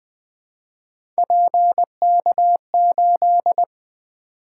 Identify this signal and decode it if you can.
Morse code, heard as a keyed tone, about 20 wpm, 700 Hz. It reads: PK8
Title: Three Letter-Number Combinations - 20wpm